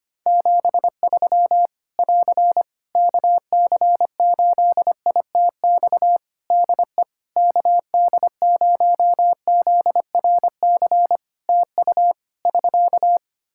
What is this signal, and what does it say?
Morse code, heard as a keyed tone, about 25 wpm, 700 Hz. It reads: 73 <AR> KC8ITX DE KD0ZRC TU <SK>
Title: Practice QSO - Exchange 4 - Ending - 25wpm